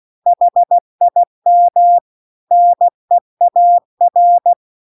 Morse code, heard as a keyed tone, about 16 wpm, 700 Hz. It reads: HIM NEAR